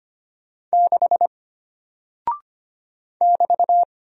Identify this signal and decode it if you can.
Morse code, heard as a keyed tone, about 25 wpm, 700 Hz. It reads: 6 <BT>
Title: Single Character (Letter:Number:Punctuation:Prosign) Alt 25wpm